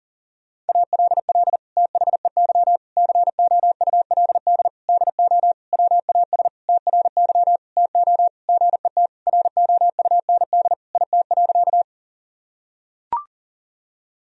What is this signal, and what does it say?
Morse code, heard as a keyed tone, about 40 wpm, 700 Hz. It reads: ALL THEY COULD DO WAS TRY TO GET ROUND IT.
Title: Sentences from Top 1000 Words - Review Part 1 - 40wpm